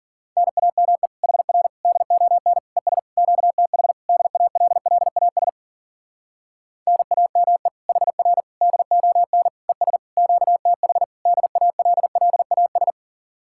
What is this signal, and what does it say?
Morse code, heard as a keyed tone, about 52 wpm, 700 Hz. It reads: NAME HR DON ES QTH DALLAS NAME HR DON ES QTH DALLAS